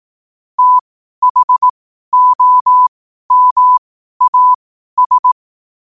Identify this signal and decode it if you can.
Morse code, heard as a keyed tone, about 18 wpm, spaced as for 12 wpm, 1000 Hz. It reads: THOMAS
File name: qotdfiles / 12wpm